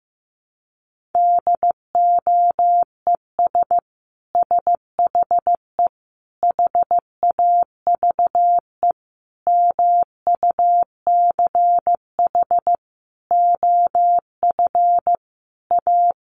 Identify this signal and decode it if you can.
Morse code, heard as a keyed tone, about 15 wpm, 700 Hz. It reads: DOES SHE HAVE MUCH OF A
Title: Sentences from Top 1400 Words 15wpm